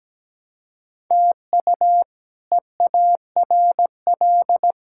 Morse code, heard as a keyed tone, about 17 wpm, 700 Hz. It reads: TU EARL
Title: K1USN SST QSO Practice - 17wpm